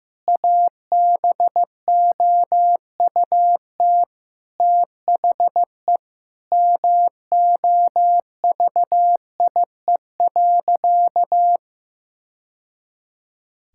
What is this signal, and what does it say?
Morse code, heard as a keyed tone, about 15 wpm, 700 Hz. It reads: ABOUT THE MOVIE.
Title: Sentences from Top 1000 Words - Review Part 5 - 15wpm